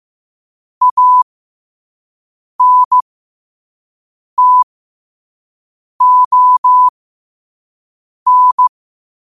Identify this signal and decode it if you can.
Morse code, heard as a keyed tone, about 15 wpm, spaced as for 5 wpm, 1000 Hz. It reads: ANTON
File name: qotdfiles / 5wpm